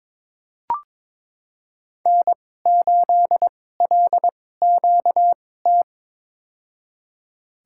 Morse code, heard as a keyed tone, about 22 wpm, spaced as for 16 wpm, 700 Hz. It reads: N8LQT
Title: USA Call Signs - 5 Characters - 2x Character Spacing - 22wpm